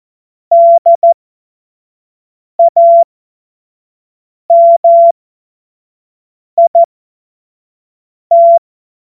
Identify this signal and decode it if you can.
Morse code, heard as a keyed tone, about 14 wpm, spaced as for 5 wpm, 680 Hz. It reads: DAMIT
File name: Morsekurs / Witze / Witz 24.Oktober 2023 25 F70